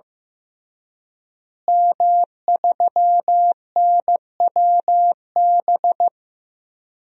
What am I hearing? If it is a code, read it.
Morse code, heard as a keyed tone, about 16 wpm, 700 Hz. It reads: M3NWB
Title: DXCC-100 Call Signs - 5 Characters - Repeated 3x - 15wpm